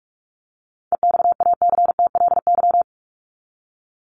Morse code, heard as a keyed tone, about 44 wpm, 700 Hz. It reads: EXACTLY
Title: LY Suffix - Top 100 Words with Base Words 45wpm